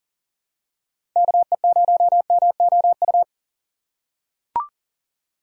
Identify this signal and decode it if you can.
Morse code, heard as a keyed tone, about 40 wpm, 700 Hz. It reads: KE0MOU
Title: USA Call Signs - 6 Characters - 3x Character Spacing - 40wpm